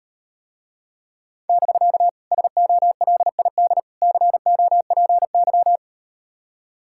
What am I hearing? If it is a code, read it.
Morse code, heard as a keyed tone, about 38 wpm, 700 Hz. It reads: <BK> SOLID COPY